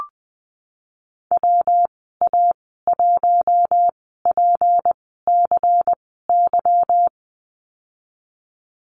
Morse code, heard as a keyed tone, about 20 wpm, spaced as for 15 wpm, 700 Hz. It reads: WA1PCY
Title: 154 Lesson 25g - Letter p - Call Signs 20wpm